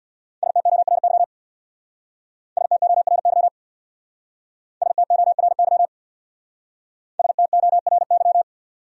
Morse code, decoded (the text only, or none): STORY STORY STORY STORY